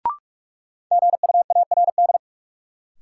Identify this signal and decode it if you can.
Morse code, heard as a keyed tone, about 42 wpm, 700 Hz. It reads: GUARD
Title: Sentences from Top 700 Words - New Word Drill - 45wpm